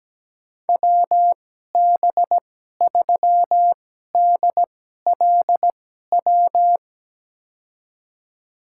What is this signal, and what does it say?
Morse code, heard as a keyed tone, about 17 wpm, spaced as for 12 wpm, 700 Hz. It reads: WB3DLW
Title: USA Call Signs - 6 Characters - 2x Character Spacing - 17wpm